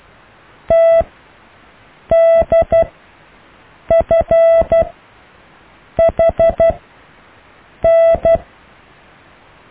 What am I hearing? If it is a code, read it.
Morse code, heard as a keyed tone, about 12 wpm, spaced as for 6 wpm, 660 Hz. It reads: TDFHN
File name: cw / EKTPV+BFHNW